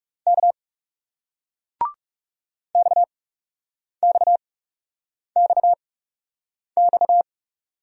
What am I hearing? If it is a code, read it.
Morse code, heard as a keyed tone, about 40 wpm, spaced as for 8 wpm, 700 Hz. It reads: K XXXX